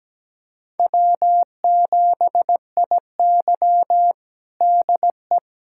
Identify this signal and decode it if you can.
Morse code, heard as a keyed tone, about 17 wpm, 700 Hz. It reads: W7IY DE
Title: USA Callsign DE Callsign - 4 Characters - 17wpm